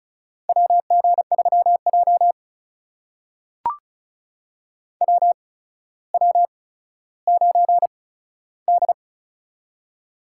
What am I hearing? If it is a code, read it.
Morse code, heard as a keyed tone, about 36 wpm, spaced as for 15 wpm, 700 Hz. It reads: WG3J WW9D
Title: USA Call Signs - 4 Characters - 8x Character Spacing - 35wpm